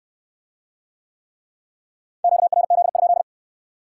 Morse code, heard as a keyed tone, about 66 wpm, 700 Hz. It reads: QRZ?